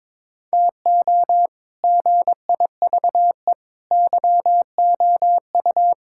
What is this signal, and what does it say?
Morse code, heard as a keyed tone, about 22 wpm, 700 Hz. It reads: TO GIVE YOU
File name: practice / 22wpm / Sets of 7 Words - 22wpm